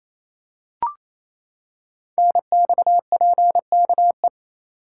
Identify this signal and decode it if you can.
Morse code, heard as a keyed tone, about 29 wpm, 700 Hz. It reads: NXPKE
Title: Code Groups - 5 Letters - 28wpm